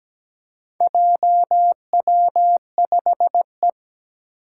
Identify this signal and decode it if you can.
Morse code, heard as a keyed tone, about 17 wpm, 700 Hz. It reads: JW5E
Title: Four Challenging Letter-Number Combinations 17wpm